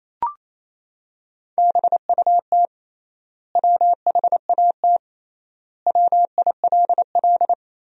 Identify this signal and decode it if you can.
Morse code, heard as a keyed tone, about 28 wpm, 700 Hz. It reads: BUT WHAT WILL
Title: 3x Word-Spacing - Sentences from Top 100 Words - 28wpm